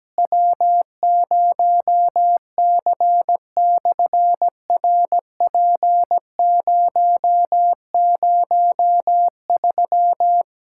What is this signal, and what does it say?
Morse code, heard as a keyed tone, about 17 wpm, 700 Hz. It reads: W0C/RP003
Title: SOTA QSO Practice 17wpm